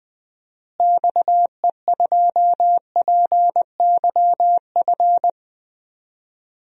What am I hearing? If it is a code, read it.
Morse code, heard as a keyed tone, about 21 wpm, 700 Hz. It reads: XE2PYF
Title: DXCC-100 Call Signs - 6 Characters - Repeated 2x - 20wpm